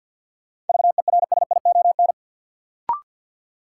Morse code, heard as a keyed tone, about 51 wpm, 700 Hz. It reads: VERSION
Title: Sentences from Top 1400 Words - New Word Drill - 50wpm